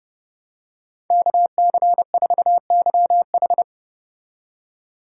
Morse code, heard as a keyed tone, about 29 wpm, 700 Hz. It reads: KC4YH